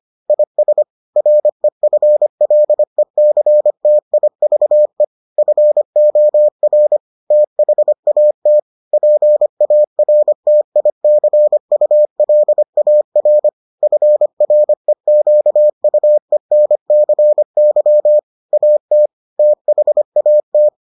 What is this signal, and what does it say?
Morse code, heard as a keyed tone, about 25 wpm, 600 Hz. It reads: IS REFLECTIVE FOR THAT PARTICULAR FREQUENCY AT THAT